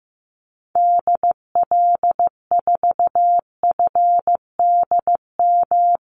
Morse code, heard as a keyed tone, about 15 wpm, 700 Hz. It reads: DL4FDM
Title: Call Signs Repeated 2x - 15wpm